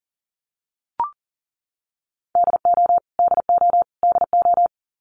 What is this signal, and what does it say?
Morse code, heard as a keyed tone, about 40 wpm, 700 Hz. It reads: DO DO DO
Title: Top 500 Words Repeated 40wpm